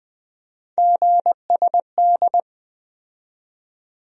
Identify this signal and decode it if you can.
Morse code, heard as a keyed tone, about 21 wpm, 700 Hz. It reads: GSD
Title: USA Call Sign Postfixes - 3 Letters 20wpm